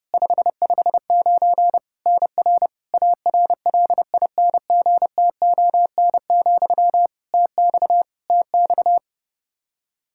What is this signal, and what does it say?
Morse code, heard as a keyed tone, about 30 wpm, 700 Hz. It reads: 559 NR ARLINGTON, TX TX
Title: Practice QSO - Exchange 1 - Beginning - 30wpm